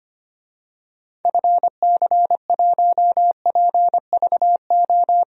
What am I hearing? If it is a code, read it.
Morse code, heard as a keyed tone, about 25 wpm, 700 Hz. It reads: FC1PVO
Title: DXCC-100 Call Signs - 6 Characters - 25wpm